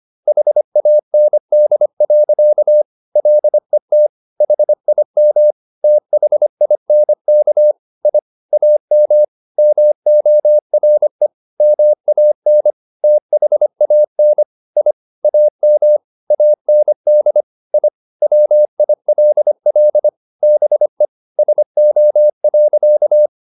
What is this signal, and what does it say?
Morse code, heard as a keyed tone, about 25 wpm, 600 Hz. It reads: HAND. LET HIM THINK I AM MORE MAN THAN I AM AND I WILL BE SO.
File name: the old man and the sea part 1 25wpm